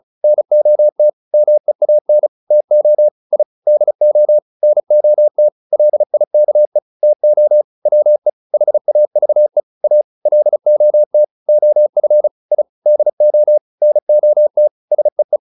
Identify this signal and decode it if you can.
Morse code, heard as a keyed tone, about 34 wpm, 600 Hz. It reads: NOT MEAN TO I DO NOT LIKE TO WE HAVE A LOT OF I DO NOT SEE
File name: WD403 35